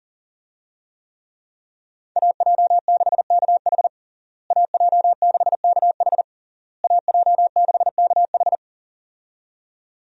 Morse code, heard as a keyed tone, about 40 wpm, 700 Hz. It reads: AJ6KH AJ6KH AJ6KH
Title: USA Call Signs - 5 Characters - Repeated 3x - 40wpm